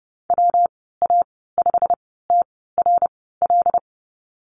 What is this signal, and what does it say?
Morse code, heard as a keyed tone, about 30 wpm, spaced as for 17 wpm, 700 Hz. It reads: WA5TRL
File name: Lesson 19g - Letter w - Call Signs 30wpm